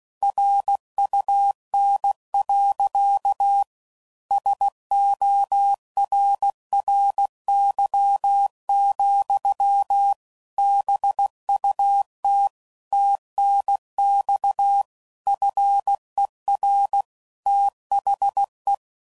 Morse code, heard as a keyed tone, about 16 wpm, 800 Hz. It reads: RUN. SORRY, BUT TNX FER THE
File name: CW Code Groups @15WPM / QSOs / kc7haa